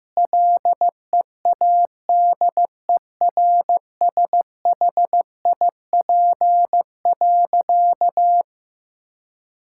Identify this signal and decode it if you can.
Morse code, heard as a keyed tone, about 15 wpm, 700 Hz. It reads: LEADERSHIP.
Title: Sentences from Top 1600 Words 15wpm